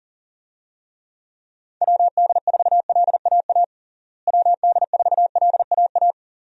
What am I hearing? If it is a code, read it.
Morse code, heard as a keyed tone, about 40 wpm, 700 Hz. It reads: WD4LAA WD4LAA
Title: USA Call Signs - 6 Characters - Repeated 3x - 40wpm